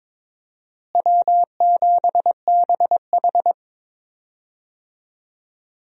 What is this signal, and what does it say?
Morse code, heard as a keyed tone, about 22 wpm, 700 Hz. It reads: W7BH
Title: USA Call Signs - 4 Characters - Repeated 2x - 22wpm